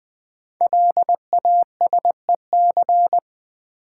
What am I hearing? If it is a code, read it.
Morse code, heard as a keyed tone, about 20 wpm, 700 Hz. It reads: LASEC
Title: Code Groups - 5 Letters - 20wpm